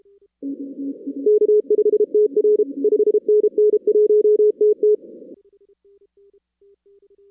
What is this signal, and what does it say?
Morse code, heard as a keyed tone, about 33 wpm, 410 Hz. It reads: K5TR 5NN 1TT
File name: 2016 arrl dx cw / 2016-ARRL-DX-CW-K5TR- - 0-17